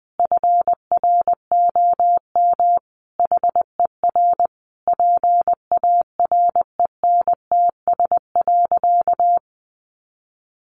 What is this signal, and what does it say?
Morse code, heard as a keyed tone, about 20 wpm, 700 Hz. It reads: FROM HER PARENTS.
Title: Easy Phrases 20wpm